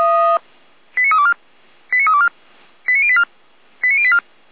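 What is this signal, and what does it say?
Five-tone selective call (ZVEI), heard as quick runs of five tones, tones of about 80 ms. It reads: 89324 89324 89084 89084
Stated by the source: ZVEI five-tone selective call received by SDR.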